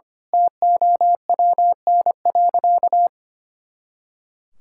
Morse code, heard as a keyed tone, about 25 wpm, 700 Hz. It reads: TOWN.
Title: Pangram Challenge v2 - 25wpm